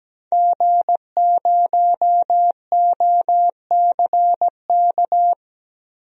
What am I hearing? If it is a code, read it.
Morse code, heard as a keyed tone, about 17 wpm, 700 Hz. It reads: G0OCK
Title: DXCC-100 Call Signs - 5 Characters - Repeated 2x - 17wpm